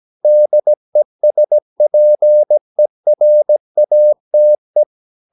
Morse code, heard as a keyed tone, about 17 wpm, 600 Hz. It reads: DESPERATE
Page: Romanen Never af Ken Follet på engelsk.